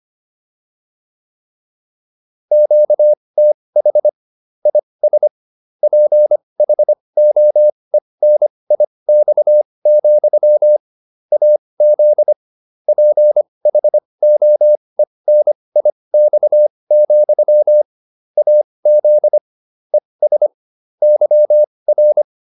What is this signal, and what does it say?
Morse code, heard as a keyed tone, about 25 wpm, spaced as for 20 wpm, 600 Hz. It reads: QTH IS PHOENIX, AZ PHOENIX, AZ ES YR